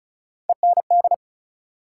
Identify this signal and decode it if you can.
Morse code, heard as a keyed tone, about 34 wpm, 700 Hz. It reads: END
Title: Sets of 2 Words - New Word Drill - 35wpm